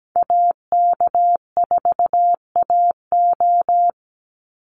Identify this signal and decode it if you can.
Morse code, heard as a keyed tone, about 17 wpm, 700 Hz. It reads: AK4AO
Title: Call Signs Repeated 3x - 17wpm